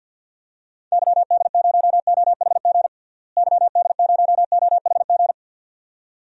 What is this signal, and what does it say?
Morse code, heard as a keyed tone, about 51 wpm, 700 Hz. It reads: YD0OHG YD0OHG
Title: DXCC-100 Call Signs - 6 Characters - Repeated 2x - 50wpm